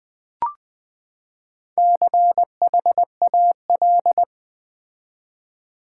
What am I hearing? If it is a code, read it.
Morse code, heard as a keyed tone, about 20 wpm, 700 Hz. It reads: CHAL
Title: Build A Word - 4 Letters At A Time - 11-Letter Words - 20wpm